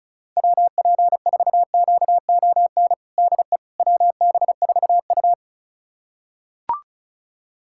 Morse code, heard as a keyed tone, about 34 wpm, 700 Hz. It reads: WP4QON DE WB4U
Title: USA Callsign DE Callsign 35wpm